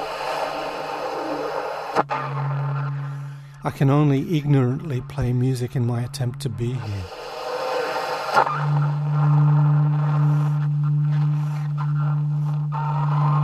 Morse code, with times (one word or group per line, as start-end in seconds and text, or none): none